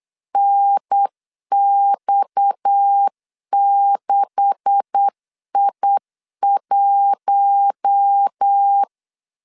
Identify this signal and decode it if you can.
Morse code, heard as a keyed tone, about 8 wpm, 800 Hz. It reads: NX6I1
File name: CW / DET / 1 9 1500C S A 8wpm 5637